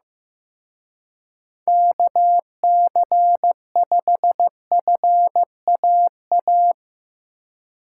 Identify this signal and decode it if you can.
Morse code, heard as a keyed tone, about 17 wpm, 700 Hz. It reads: KC5FAA
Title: Speed Racing - USA 2x3 Call Signs 15wpm